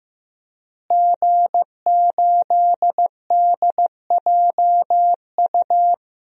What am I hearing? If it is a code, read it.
Morse code, heard as a keyed tone, about 16 wpm, 700 Hz. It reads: G8DJU
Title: DXCC-100 Call Signs - 5 Characters - Repeated 3x - 15wpm